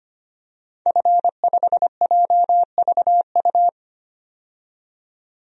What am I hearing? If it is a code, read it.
Morse code, heard as a keyed tone, about 25 wpm, 700 Hz. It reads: F5JVU
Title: DXCC-100 Call Signs - 5 Characters - 25wpm